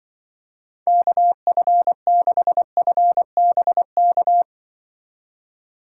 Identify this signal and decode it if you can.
Morse code, heard as a keyed tone, about 25 wpm, 700 Hz. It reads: KF6FBK
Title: Speed Racing - USA 2x3 Call Signs 20wpm